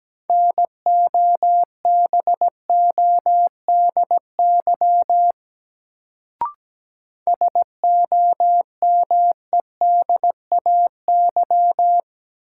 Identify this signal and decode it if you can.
Morse code, heard as a keyed tone, about 17 wpm, 700 Hz. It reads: NOBODY SOMEDAY
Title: Sentences from Top 1600 Words - New Word Drill Alt - 17wpm